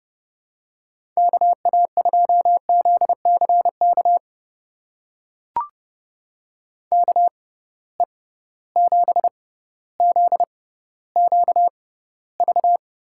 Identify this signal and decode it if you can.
Morse code, heard as a keyed tone, about 30 wpm, spaced as for 15 wpm, 700 Hz. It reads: KA2ZCK KE7ZQV
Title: USA Call Signs - 6 Characters - 6x Character Spacing - 30wpm